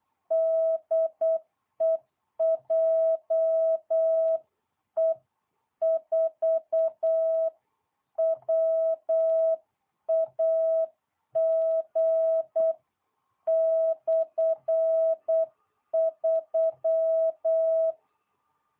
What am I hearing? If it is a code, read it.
Morse code, heard as a keyed tone, about 8 wpm, 660 Hz. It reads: DE JE4WAG/3